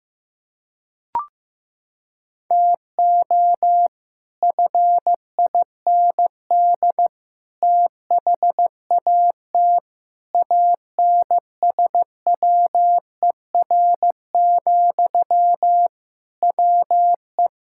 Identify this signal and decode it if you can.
Morse code, heard as a keyed tone, about 16 wpm, 700 Hz. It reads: TO FIND THAT ANSWER, WE
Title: Sentences from Top 1600 Words 15wpm